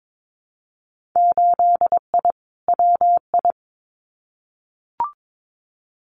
Morse code, heard as a keyed tone, about 22 wpm, 700 Hz. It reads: 8I WI
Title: Winter Field Day Exchanges 22wpm